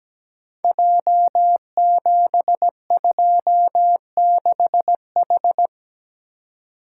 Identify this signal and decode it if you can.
Morse code, heard as a keyed tone, about 17 wpm, 700 Hz. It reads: J726H
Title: Code Groups - 5 Letter:Numbers - 17wpm